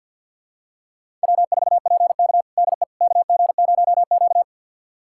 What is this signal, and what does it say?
Morse code, heard as a keyed tone, about 47 wpm, 700 Hz. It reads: W4PK DE KG0Q
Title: USA Callsign DE Callsign - 4 Characters - 50wpm